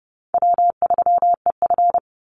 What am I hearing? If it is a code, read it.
Morse code, heard as a keyed tone, about 30 wpm, 700 Hz. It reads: W3EF
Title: Easy Call Signs Repeated 2x - 30wpm